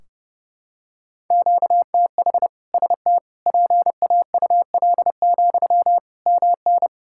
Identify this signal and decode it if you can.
Morse code, heard as a keyed tone, about 30 wpm, 700 Hz. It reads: QTH ST PAUL, MN
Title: K1USN SST QSO Practice 30wpm